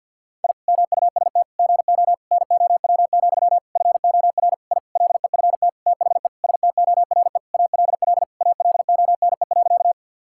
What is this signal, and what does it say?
Morse code, heard as a keyed tone, about 50 wpm, 700 Hz. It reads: I MUST GO NOW, FOR I LEFT THE STORE ALL ALONE.
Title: Sentences from Top 1000 Words - Review Part 5 - 50wpm